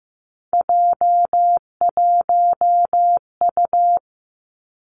Morse code, heard as a keyed tone, about 15 wpm, 700 Hz. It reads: J1U
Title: Three Challenging Letter-Number Combinations 15wpm